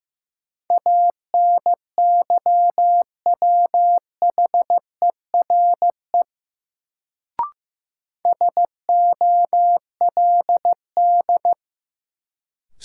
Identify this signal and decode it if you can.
Morse code, heard as a keyed tone, about 15 wpm, 700 Hz. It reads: ANYWHERE SOLD
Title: Sentences from Top 1400 Words - New Word Drill Alt - 15wpm